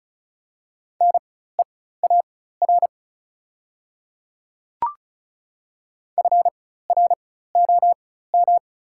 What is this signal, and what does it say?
Morse code, heard as a keyed tone, about 36 wpm, spaced as for 16 wpm, 690 Hz. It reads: NEAR FROM
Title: 4x Character Spacing - Top 500 Words - 35wpm